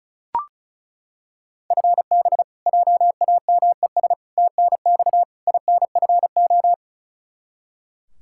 Morse code, heard as a keyed tone, about 36 wpm, 700 Hz. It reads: FB JAMES TNX INFO
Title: Practice QSO - Exchange 3 - More Info - 35wpm